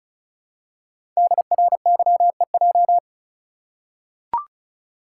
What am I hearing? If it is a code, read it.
Morse code, heard as a keyed tone, about 36 wpm, 700 Hz. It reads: DRYEJ
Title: Code Groups - 5 Letters - 35wpm